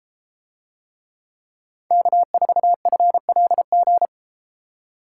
Morse code, heard as a keyed tone, about 33 wpm, 700 Hz. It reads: K4FLG